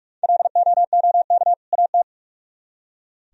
Morse code, heard as a keyed tone, about 58 wpm, 700 Hz. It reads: LOOK AT